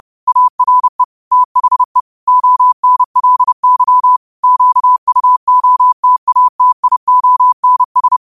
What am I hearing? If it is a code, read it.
Morse code, heard as a keyed tone, about 30 wpm, 1000 Hz. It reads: ARE THE ONLY QUOTATIONS